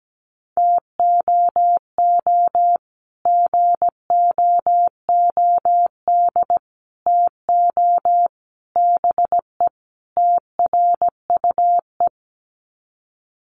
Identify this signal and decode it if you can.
Morse code, heard as a keyed tone, about 17 wpm, 690 Hz. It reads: TOO GOOD TO BE TRUE
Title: Sets of 5 Words Encore 17wpm